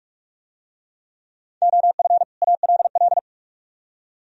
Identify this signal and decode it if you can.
Morse code, heard as a keyed tone, about 44 wpm, 700 Hz. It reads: OF ALL